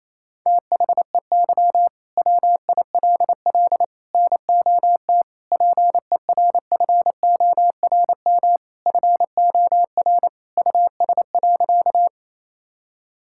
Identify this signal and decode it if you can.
Morse code, heard as a keyed tone, about 28 wpm, 700 Hz. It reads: THEY WILL NOT PERFORM FOR US.
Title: Sentences from Top 1000 Words - Review Part 5 - 28wpm